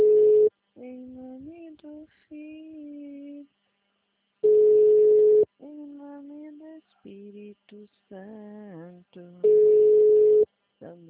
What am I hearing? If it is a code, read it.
Ringback tone (Europe, 425 Hz).